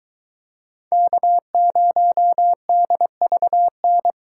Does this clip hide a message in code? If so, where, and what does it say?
Yes: Morse code from 0.9 to 4.1 s, heard as a keyed tone, about 22 wpm, 700 Hz. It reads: K0DVN